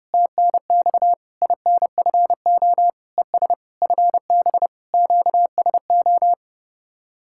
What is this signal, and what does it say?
Morse code, heard as a keyed tone, about 33 wpm, 700 Hz. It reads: TNX INFO ES FB QSO